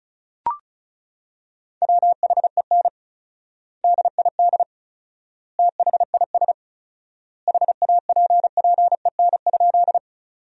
Morse code, heard as a keyed tone, about 36 wpm, 700 Hz. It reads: WHEN DID THIS HAPPEN?
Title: 4x Word-Spacing - Sentences from Top 400 Words - 35wpm